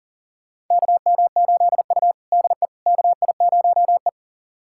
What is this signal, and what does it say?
Morse code, heard as a keyed tone, about 40 wpm, 700 Hz. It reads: KM8U DE KI0E
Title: USA Callsign DE Callsign - 4 Characters - 40wpm